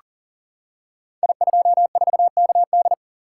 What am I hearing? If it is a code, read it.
Morse code, heard as a keyed tone, about 40 wpm, 700 Hz. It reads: I24KD
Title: Code Groups - 5 Letter:Numbers - 40wpm